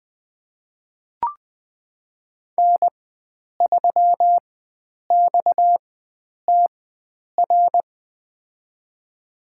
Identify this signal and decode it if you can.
Morse code, heard as a keyed tone, about 20 wpm, spaced as for 9 wpm, 700 Hz. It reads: N3XTR
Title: USA Call Signs - 5 Characters - 4x Character Spacing - 20wpm